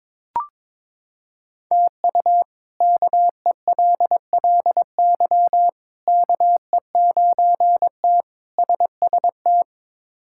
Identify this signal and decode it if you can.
Morse code, heard as a keyed tone, about 22 wpm, 700 Hz. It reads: TU KELLY KE9T SST
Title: K1USN SST QSO Practice - 22wpm